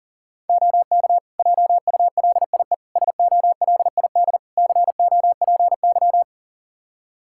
Morse code, heard as a keyed tone, about 40 wpm, 700 Hz. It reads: OK JULIE SOLID COPY
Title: Practice QSO - Exchange 3 - More Info - 40wpm